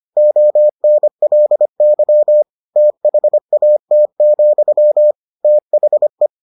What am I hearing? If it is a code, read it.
Morse code, heard as a keyed tone, about 25 wpm, 600 Hz. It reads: ONLY THAT, THE